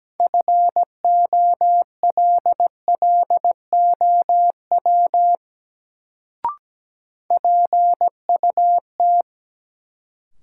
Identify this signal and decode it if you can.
Morse code, heard as a keyed tone, about 17 wpm, 700 Hz. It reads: FOLLOW PUT
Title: Sentences from Top 200 Words - New Word Drill - 17wpm